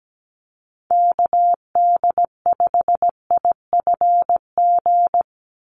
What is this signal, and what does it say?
Morse code, heard as a keyed tone, about 17 wpm, 700 Hz. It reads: KD5IFG
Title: USA Call Signs 17wpm